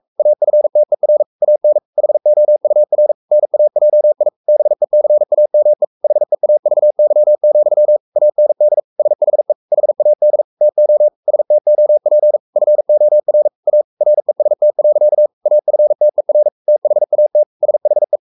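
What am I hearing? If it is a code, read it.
Morse code, heard as a keyed tone, about 44 wpm, 600 Hz. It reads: AFTER AN HOUR NAJI BECAME HEAVY, AND SHE HAD TO STOP FOR A REST. AFTER THAT SHE
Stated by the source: Morse code at 43 wpm sending Ken Follett's novel Never in English, from CW med Gnister.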